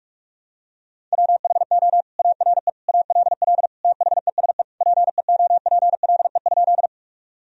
Morse code, heard as a keyed tone, about 42 wpm, 700 Hz. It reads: WHO ARE ALL THESE PEOPLE?
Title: Sentences from Top 1000 Words - Review Part 1 - 45wpm